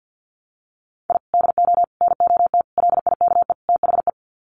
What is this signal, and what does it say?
Morse code, heard as a keyed tone, about 50 wpm, 700 Hz. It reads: I DO NOT LIKE THE
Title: Sets of 5 Words Encore 50wpm